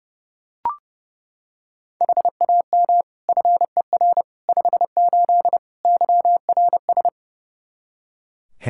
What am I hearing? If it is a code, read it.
Morse code, heard as a keyed tone, about 30 wpm, 700 Hz. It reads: HAM FER 58 YRS
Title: Practice QSO - Exchange 3 - More Info - 30wpm